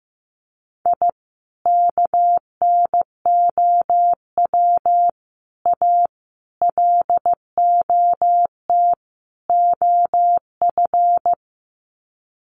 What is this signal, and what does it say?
Morse code, heard as a keyed tone, about 15 wpm, 700 Hz. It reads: I KNOW A LOT OF
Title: Sets of 5 Words Encore 15wpm